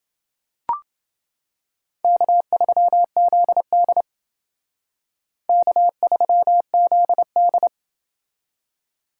Morse code, heard as a keyed tone, about 29 wpm, 700 Hz. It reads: K3ZD K3ZD